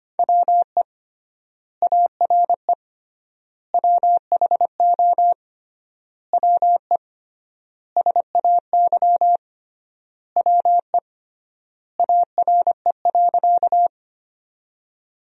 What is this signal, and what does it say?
Morse code, heard as a keyed tone, about 25 wpm, 700 Hz. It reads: WE ARE WHO WE SAY WE ARE.
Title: 3x Word-Spacing - Sentences from Top 200 Words - 25wpm